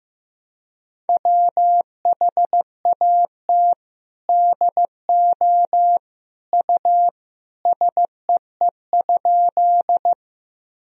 Morse code, heard as a keyed tone, about 15 wpm, 700 Hz. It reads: WHAT DO U SEE?
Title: Sentences from Top 100 Words and Abbreviations 15wpm